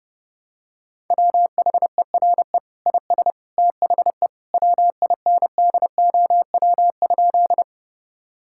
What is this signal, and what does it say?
Morse code, heard as a keyed tone, about 30 wpm, 700 Hz. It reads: WHERE IS THE WINDOW?
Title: Sentences from Top 1600 Words 30wpm